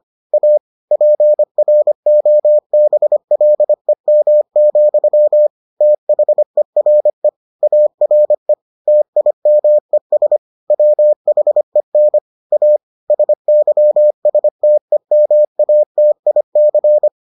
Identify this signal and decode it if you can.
Morse code, heard as a keyed tone, about 25 wpm, 600 Hz. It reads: A PROBLEM, THERE ARE TIMES WHEN A SYSTEMATIC